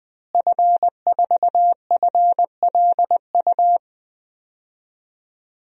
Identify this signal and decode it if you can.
Morse code, heard as a keyed tone, about 20 wpm, 700 Hz. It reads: F4FLU
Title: DXCC-100 Call Signs - 5 Characters - Repeated 3x - 20wpm